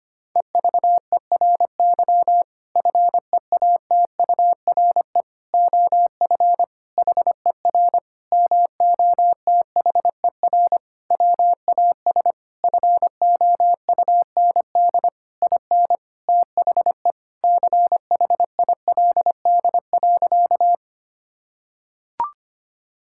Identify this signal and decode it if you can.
Morse code, heard as a keyed tone, about 25 wpm, 700 Hz. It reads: EVERY FEATURE OF HER MOTHER WAS FOUND IN THE CHILD.
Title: Sentences from Top 1000 Words - Review Part 5 - 25wpm